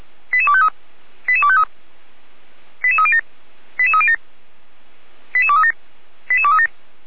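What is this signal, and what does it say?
Five-tone selective call (ZVEI), heard as quick runs of five tones, tones of about 80 ms. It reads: 88253 88253 88307 88307 88237 88237
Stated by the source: ZVEI five-tone call received on SDR.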